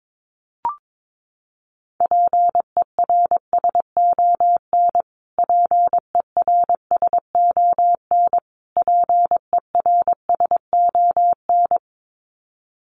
Morse code, heard as a keyed tone, about 22 wpm, 700 Hz. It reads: PERSON PERSON PERSON
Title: Top 500 Words Repeated 22wpm